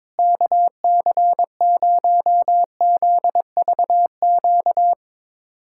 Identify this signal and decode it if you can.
Morse code, heard as a keyed tone, about 22 wpm, 700 Hz. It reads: KC0ZVQ
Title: USA Callsign DE Callsign 22wpm